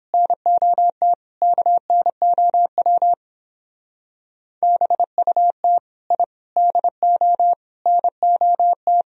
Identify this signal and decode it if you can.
Morse code, heard as a keyed tone, about 26 wpm, 700 Hz. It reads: NOT KNOW BUT I DO NOT
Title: Speed-Racing - Sets of 5 Words 22wpm